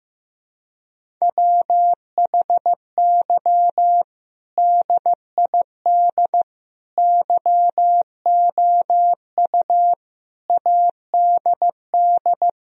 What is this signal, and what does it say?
Morse code, heard as a keyed tone, about 16 wpm, 700 Hz. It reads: WHY DID YOU ADD
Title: Sentences from Top 1000 Words - Review Part 1 - 15wpm